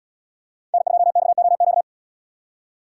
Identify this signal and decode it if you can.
Morse code, heard as a keyed tone, about 70 wpm, 700 Hz. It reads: N2XOQ